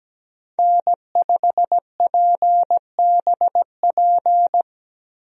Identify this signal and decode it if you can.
Morse code, heard as a keyed tone, about 17 wpm, 700 Hz. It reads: N5PBP
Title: USA Call Signs - 5 Characters - Repeated 3x - 17wpm